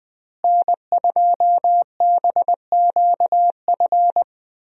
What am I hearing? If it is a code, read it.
Morse code, heard as a keyed tone, about 20 wpm, 700 Hz. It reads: N2BQF
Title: USA Call Signs - 5 Characters - Repeated 3x - 20wpm